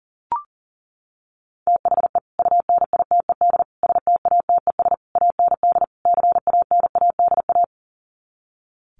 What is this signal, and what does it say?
Morse code, heard as a keyed tone, about 40 wpm, 700 Hz. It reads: THE UNITED STATES AND CANADA
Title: Sets of 5 Words 40wpm